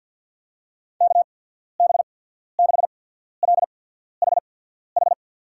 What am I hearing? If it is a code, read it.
Morse code, heard as a keyed tone, about 53 wpm, spaced as for 14 wpm, 700 Hz. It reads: KB6LHH
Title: USA Call Signs - 6 Characters - 8x Character Spacing - 50wpm